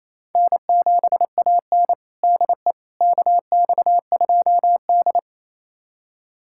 Morse code, heard as a keyed tone, about 28 wpm, 700 Hz. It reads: N7AN DE KX2D
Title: USA Callsign DE Callsign - 4 Characters - 28wpm